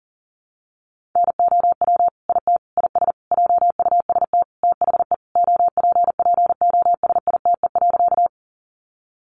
Morse code, heard as a keyed tone, about 40 wpm, 700 Hz. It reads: NOW IT IS JUST THE OPPOSITE.
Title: Sentences from Top 1300 Words 40wpm